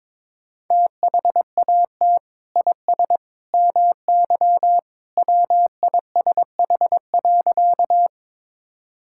Morse code, heard as a keyed tone, about 22 wpm, 700 Hz. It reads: THAT IS MY WISH.
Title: Sentences from Top 1600 Words 22wpm